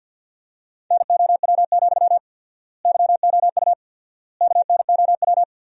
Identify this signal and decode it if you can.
Morse code, heard as a keyed tone, about 53 wpm, 700 Hz. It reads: NOW, YOU KNOW
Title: 4x Word-Spacing - Sentences from Top 200 Words - 50wpm